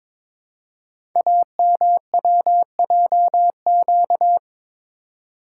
Morse code, heard as a keyed tone, about 22 wpm, 700 Hz. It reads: AMWJQ
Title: Code Groups - 5 Letters - 22wpm